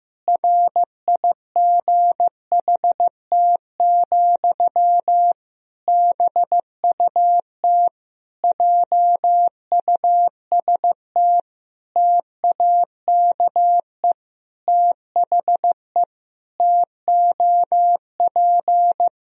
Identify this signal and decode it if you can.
Morse code, heard as a keyed tone, about 15 wpm, 700 Hz. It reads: RIGHT, BUT JUST TAKE THE TOP
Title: Sentences from Top 1000 Words - Review Part 2 - 15wpm